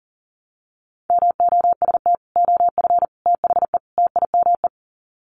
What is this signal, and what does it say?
Morse code, heard as a keyed tone, about 40 wpm, 700 Hz. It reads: MOST OF THE TIME
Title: Sets of 4 Words 40wpm